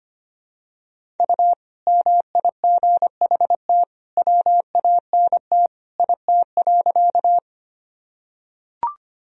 Morse code, heard as a keyed tone, about 25 wpm, 700 Hz. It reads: U MIGHT WANT IT.
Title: Sentences from Top 300 Words and Abbreviations 25wpm